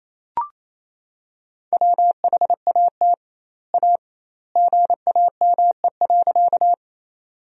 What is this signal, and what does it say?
Morse code, heard as a keyed tone, about 28 wpm, 700 Hz. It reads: WHAT A GAME.
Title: 2x Word-Spacing - Sentences from Top 500 Words - 28wpm